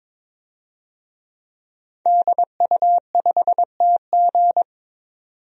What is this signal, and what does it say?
Morse code, heard as a keyed tone, about 22 wpm, 700 Hz. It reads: DU5TG
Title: DXCC-100 Call Signs - 5 Characters - 22wpm